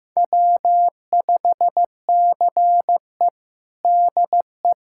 Morse code, heard as a keyed tone, about 15 wpm, 700 Hz. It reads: W5CE DE
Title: USA Callsign DE Callsign - 4 Characters - 15wpm